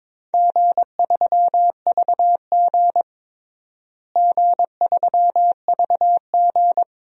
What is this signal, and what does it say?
Morse code, heard as a keyed tone, about 22 wpm, 700 Hz. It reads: G3VG G3VG
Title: DXCC-100 Call Signs - 4 Characters - Repeated 3x - 22wpm